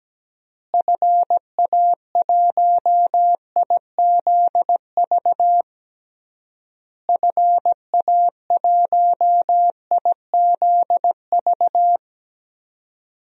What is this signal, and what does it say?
Morse code, heard as a keyed tone, about 17 wpm, 700 Hz. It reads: FA1IZV FA1IZV
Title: DXCC-100 Call Signs - 6 Characters - Repeated 3x - 17wpm